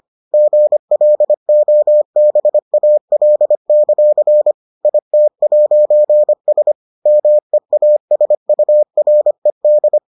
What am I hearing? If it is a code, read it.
Morse code, heard as a keyed tone, about 25 wpm, 600 Hz. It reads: GLOBAL; IT'S MEASURED